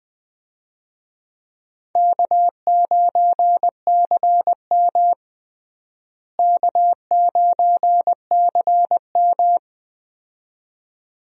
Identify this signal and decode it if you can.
Morse code, heard as a keyed tone, about 20 wpm, 700 Hz. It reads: K9CM K9CM
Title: USA Call Signs - 4 Characters - Repeated 2x - 20wpm